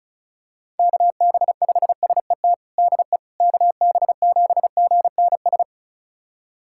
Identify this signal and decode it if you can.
Morse code, heard as a keyed tone, about 34 wpm, 700 Hz. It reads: KB5SET DE KB7GNS
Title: USA Callsign DE Callsign - 6 Characters - 35wpm